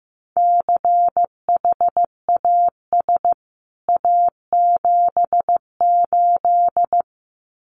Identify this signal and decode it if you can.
Morse code, heard as a keyed tone, about 15 wpm, 700 Hz. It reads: CHAS A78
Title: CWT Exchange 15wpm